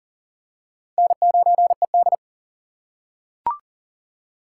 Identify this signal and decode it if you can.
Morse code, heard as a keyed tone, about 40 wpm, 700 Hz. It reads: N9ED
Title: USA Call Signs - 4 Characters - 4x Character Spacing - 40wpm